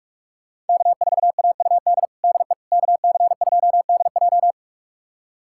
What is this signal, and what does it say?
Morse code, heard as a keyed tone, about 46 wpm, 700 Hz. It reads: K4AUD DE KC2DJ
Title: USA Callsign DE Callsign - 5 Characters - 45wpm